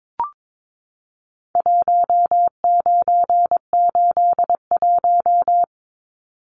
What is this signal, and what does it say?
Morse code, heard as a keyed tone, about 22 wpm, 700 Hz. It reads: 1981
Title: Numbers - Contest Style - 4-Digit 22wpm